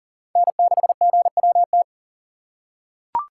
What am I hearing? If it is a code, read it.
Morse code, heard as a keyed tone, about 40 wpm, 700 Hz. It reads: N6GWT
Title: USA Call Signs - 5 Characters - 3x Character Spacing - 40wpm